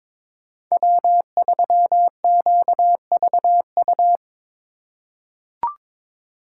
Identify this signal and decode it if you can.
Morse code, heard as a keyed tone, about 22 wpm, 700 Hz. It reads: W3QVU
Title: USA Call Signs - 5 Characters - 4x Character Spacing - 22wpm